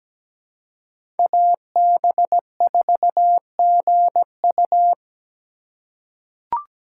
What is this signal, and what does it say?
Morse code, heard as a keyed tone, about 17 wpm, 700 Hz. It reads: AB4GU
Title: USA Call Signs - 5 Characters - 4x Character Spacing - 17wpm